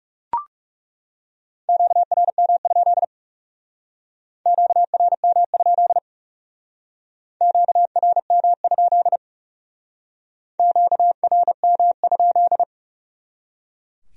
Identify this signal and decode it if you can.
Morse code, heard as a keyed tone, about 38 wpm, 700 Hz. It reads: QRM? QRM? QRM? QRM?